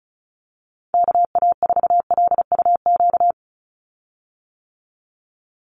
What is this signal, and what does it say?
Morse code, heard as a keyed tone, about 34 wpm, 700 Hz. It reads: KA4LUQ
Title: USA Call Signs 35wpm